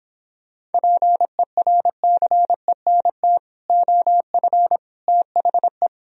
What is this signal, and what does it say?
Morse code, heard as a keyed tone, about 29 wpm, 700 Hz. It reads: PERCENT OF THE